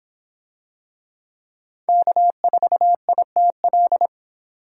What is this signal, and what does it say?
Morse code, heard as a keyed tone, about 26 wpm, 700 Hz. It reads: K4ITL